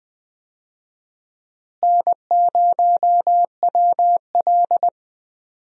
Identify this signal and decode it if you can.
Morse code, heard as a keyed tone, about 20 wpm, 700 Hz. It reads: N0WL
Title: USA Call Signs - 4 Characters - Repeated 2x - 20wpm